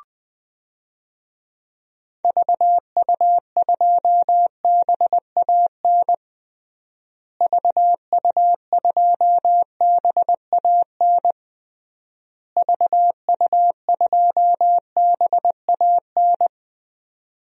Morse code, heard as a keyed tone, about 20 wpm, 700 Hz. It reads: VU2BAN VU2BAN VU2BAN
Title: DXCC-100 Call Signs - 6 Characters - Repeated 3x - 20wpm